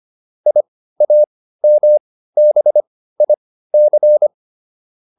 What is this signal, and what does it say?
Morse code, heard as a keyed tone, about 25 wpm, spaced as for 15 wpm, 600 Hz. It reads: IAMBIC